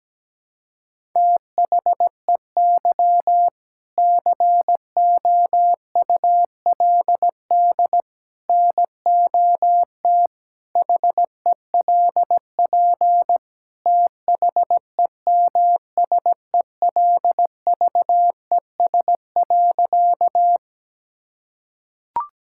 Morse code, heard as a keyed tone, about 17 wpm, 700 Hz. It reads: THEY COULD NOT HELP THEMSELVES.
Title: Sentences from Top 1000 Words - Review Part 5 - 17wpm